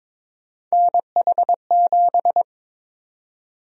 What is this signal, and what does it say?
Morse code, heard as a keyed tone, about 22 wpm, 700 Hz. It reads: NH7
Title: USA Call Sign Prefixes - Repeated 2x - 22wpm